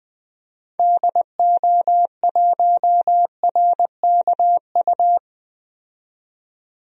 Morse code, heard as a keyed tone, about 20 wpm, 700 Hz. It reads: DO1RKU
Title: DXCC-100 Call Signs - 6 Characters - 20wpm